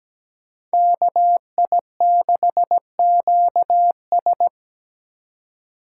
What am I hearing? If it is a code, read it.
Morse code, heard as a keyed tone, about 20 wpm, 700 Hz. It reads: KI6QS
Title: Speed Racing - USA 2x2 Call Signs 17wpm